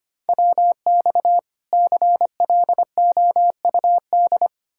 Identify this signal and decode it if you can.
Morse code, heard as a keyed tone, about 33 wpm, 700 Hz. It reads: WX CLOUD